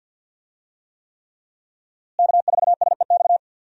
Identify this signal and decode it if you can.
Morse code, heard as a keyed tone, about 50 wpm, 700 Hz. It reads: K4SEX
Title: USA Call Signs - 5 Characters - Repeated 3x - 50wpm